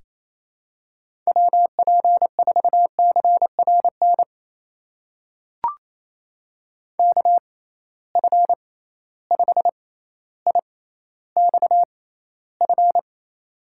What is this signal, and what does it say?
Morse code, heard as a keyed tone, about 28 wpm, spaced as for 14 wpm, 700 Hz. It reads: WP4CRN KF5IXF
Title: USA Call Signs - 6 Characters - 6x Character Spacing - 28wpm